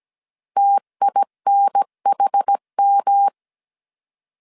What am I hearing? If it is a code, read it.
Morse code, heard as a keyed tone, about 17 wpm, 800 Hz. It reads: TINHM
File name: CW / DET / 1 3 1500C S A 16wpm 2141